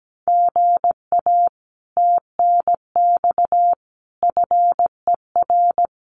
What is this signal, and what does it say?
Morse code, heard as a keyed tone, about 17 wpm, 700 Hz. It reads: GA TNX FER
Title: Common QSO Phrases Encore 17wpm